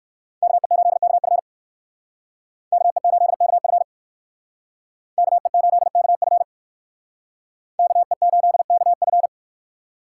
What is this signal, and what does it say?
Morse code, heard as a keyed tone, about 59 wpm, 700 Hz. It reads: KE8KF KE8KF KE8KF KE8KF